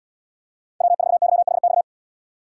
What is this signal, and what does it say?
Morse code, heard as a keyed tone, about 77 wpm, 700 Hz. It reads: W4YHX